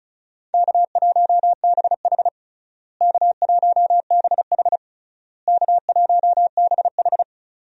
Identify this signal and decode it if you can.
Morse code, heard as a keyed tone, about 34 wpm, 700 Hz. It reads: K1BH K1BH K1BH
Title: USA Call Signs - 4 Characters - Repeated 3x - 35wpm